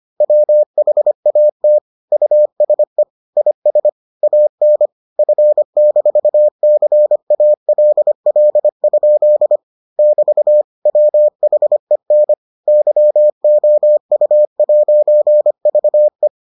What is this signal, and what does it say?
Morse code, heard as a keyed tone, about 25 wpm, 600 Hz. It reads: WHAT USE IS AN F-CALL? <BT> WHEN YOU'VE